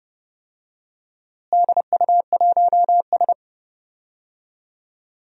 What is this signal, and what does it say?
Morse code, heard as a keyed tone, about 30 wpm, 700 Hz. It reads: DU1S
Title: DXCC-100 Call Signs - 4 Characters - Repeated 2x - 30wpm